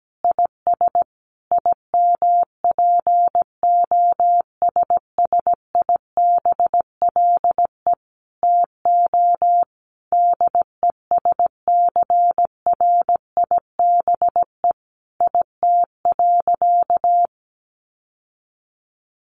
Morse code, heard as a keyed tone, about 17 wpm, 700 Hz. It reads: IS IMPOSSIBLE TO DESCRIBE IT.
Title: Sentences from Top 1100 Words 17wpm